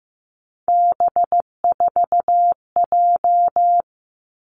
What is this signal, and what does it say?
Morse code, heard as a keyed tone, about 15 wpm, 700 Hz. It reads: B4J
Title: Three Challenging Letter-Number Combinations 15wpm